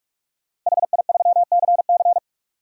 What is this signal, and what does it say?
Morse code, heard as a keyed tone, about 46 wpm, 700 Hz. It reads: HI3CC
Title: DXCC-100 Call Signs - 5 Characters - Repeated 3x - 45wpm